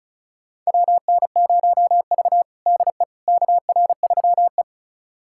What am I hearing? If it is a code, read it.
Morse code, heard as a keyed tone, about 36 wpm, 700 Hz. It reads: WN0V DE KR3E
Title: USA Callsign DE Callsign - 4 Characters - 35wpm